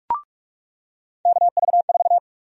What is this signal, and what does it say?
Morse code, heard as a keyed tone, about 45 wpm, 700 Hz. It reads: KV4
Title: USA Call Sign Prefixes - 45wpm